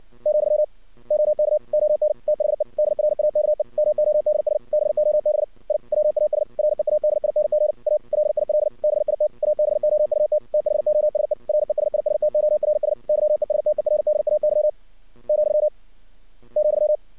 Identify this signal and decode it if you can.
Morse code, heard as a keyed tone, about 59 wpm, 610 Hz. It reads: , OK ON THE DRAKE MODS BOB I WAS READING A LIL BIT ABOUT THOSE RESISTORS YESTERDAY , ,
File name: archived / CRSnet